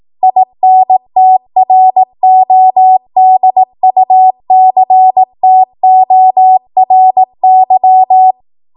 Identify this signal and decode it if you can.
Morse code, heard as a keyed tone, about 18 wpm, 750 Hz. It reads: INTRODUCTORY
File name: Morse / 18WPM